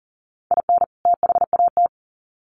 Morse code, heard as a keyed tone, about 40 wpm, 700 Hz. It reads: IN THAT
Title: Sets of 2 Words 40wpm